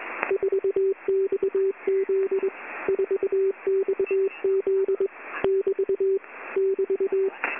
Morse code, heard as a keyed tone, about 21 wpm, 390 Hz. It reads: 4XZ 4XZ <BT> <BT>
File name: Short Wave Radio Noise 4